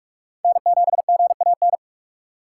Morse code, heard as a keyed tone, about 44 wpm, 700 Hz. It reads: N7GAN
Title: USA Call Signs - 6x Character Spacing - 45wpm